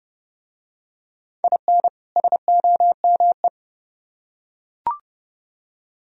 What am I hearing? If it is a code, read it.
Morse code, heard as a keyed tone, about 31 wpm, 700 Hz. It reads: IN SOME